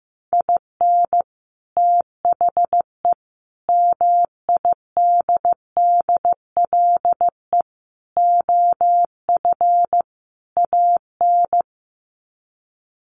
Morse code, heard as a keyed tone, about 15 wpm, 700 Hz. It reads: IN THE MIDDLE OF AN
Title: Sets of 5 Words Encore2 15wpm